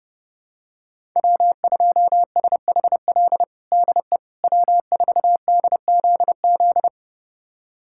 Morse code, heard as a keyed tone, about 30 wpm, 700 Hz. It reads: W2SHL DE W4DZZ
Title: USA Callsign DE Callsign - 5 Characters - 30wpm